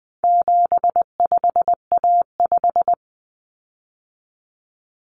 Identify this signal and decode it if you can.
Morse code, heard as a keyed tone, about 20 wpm, 700 Hz. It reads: 75A5
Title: Common Cut Numbers - 4-Digit 20wpm